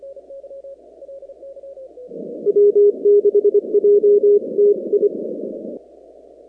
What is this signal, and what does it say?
Morse code, heard as a keyed tone, about 25 wpm, 410 Hz. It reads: W6JTI
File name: contests / 2019-cw-wpx-kt5j-k5pi- -2330 0-19